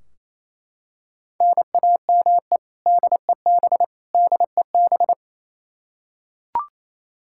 Morse code, heard as a keyed tone, about 28 wpm, 700 Hz. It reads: NAME DEB DEB
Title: Practice QSO - Exchange 1 - Beginning - 28wpm